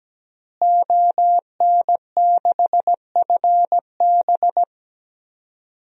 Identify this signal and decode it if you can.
Morse code, heard as a keyed tone, about 17 wpm, 700 Hz. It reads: ON6FB
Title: DXCC-100 Call Signs - 5 Characters - Repeated 3x - 17wpm